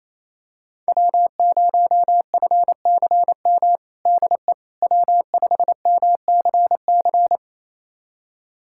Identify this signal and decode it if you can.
Morse code, heard as a keyed tone, about 28 wpm, 700 Hz. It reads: W0FCM DE W5MCC
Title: USA Callsign DE Callsign - 5 Characters - 28wpm